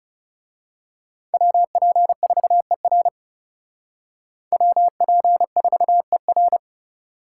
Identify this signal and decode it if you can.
Morse code, heard as a keyed tone, about 34 wpm, 700 Hz. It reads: WP4ER WP4ER
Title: Speed Racing - USA 2x2 Call Signs 30wpm